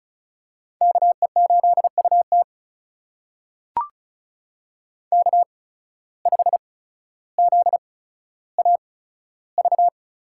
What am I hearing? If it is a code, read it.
Morse code, heard as a keyed tone, about 36 wpm, spaced as for 15 wpm, 700 Hz. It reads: KE8UT K5ZAV
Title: USA Call Signs - 5 Characters - 8x Character Spacing - 35wpm